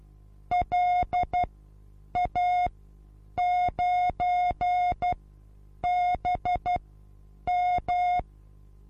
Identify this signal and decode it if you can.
Morse code, heard as a keyed tone, about 12 wpm, spaced as for 8 wpm, 720 Hz. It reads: LA9BM
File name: Onlinelehrgang / morsen / DJ4UF Morsen23